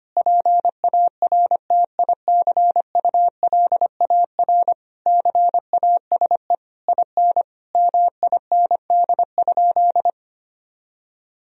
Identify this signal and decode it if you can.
Morse code, heard as a keyed tone, about 25 wpm, 700 Hz. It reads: PARTICULAR CASE IN MIND?
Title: Sentences from Top 1000 Words - Review Part 4 - 25wpm